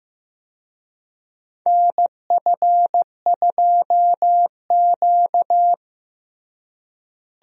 Morse code, heard as a keyed tone, about 15 wpm, 700 Hz. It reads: NF2Q
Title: USA Call Signs - 4 Characters - Repeated 3x - 15wpm